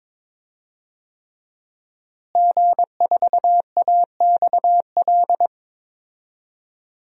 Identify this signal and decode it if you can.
Morse code, heard as a keyed tone, about 22 wpm, 700 Hz. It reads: G4AXL
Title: DXCC-100 Call Signs - 5 Characters - Repeated 3x - 22wpm